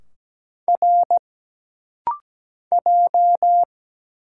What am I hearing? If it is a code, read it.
Morse code, heard as a keyed tone, about 17 wpm, 700 Hz. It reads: R J
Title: Single Letter-Number Alt - Rapid-Fire 17wpm